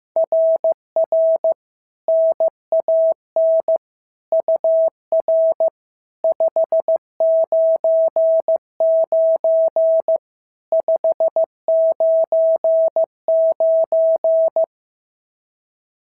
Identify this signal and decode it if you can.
Morse code, heard as a keyed tone, about 15 wpm, 650 Hz. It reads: RR NAN UR 599 599
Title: SOTA QSO Practice 15wpm